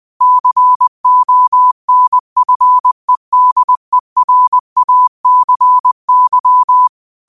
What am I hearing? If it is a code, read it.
Morse code, heard as a keyed tone, about 20 wpm, 1000 Hz. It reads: CONFEDERACY
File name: qotdfiles / 20wpm